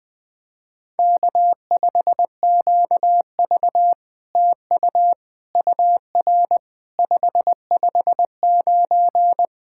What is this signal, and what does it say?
Morse code, heard as a keyed tone, about 20 wpm, 700 Hz. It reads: K5QV TU UR 559
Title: SOTA QSO Practice 20wpm